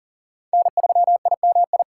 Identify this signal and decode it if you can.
Morse code, heard as a keyed tone, about 40 wpm, 700 Hz. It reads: N3IMI
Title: USA Call Signs - 4x Character Spacing - 40wpm